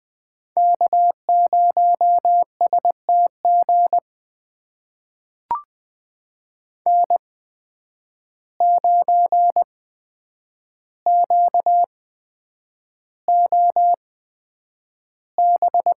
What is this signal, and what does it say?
Morse code, heard as a keyed tone, about 20 wpm, spaced as for 9 wpm, 700 Hz. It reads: K0STG N9QOB
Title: USA Call Signs - 5 Characters - 8x Character Spacing - 20wpm